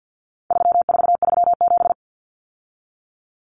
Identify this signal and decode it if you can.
Morse code, heard as a keyed tone, about 50 wpm, 700 Hz. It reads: 3437
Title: Common Cut Numbers - 4-Digit 50wpm